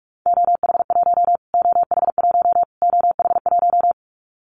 Morse code, heard as a keyed tone, about 44 wpm, 700 Hz. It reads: OH1 OH1 OH1
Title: Callsign Prefixes for Top 25 DXCC - Repeated 3x - 45wpm